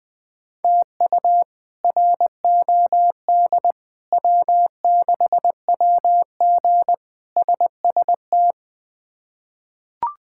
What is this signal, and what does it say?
Morse code, heard as a keyed tone, about 20 wpm, 700 Hz. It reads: TU ROD W6WG SST
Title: K1USN SST QSO Practice - 20wpm